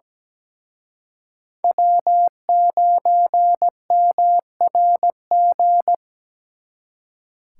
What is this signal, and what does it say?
Morse code, heard as a keyed tone, about 20 wpm, 700 Hz. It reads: W9MRG
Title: Speed Racing - USA 1x3 Call Signs 17wpm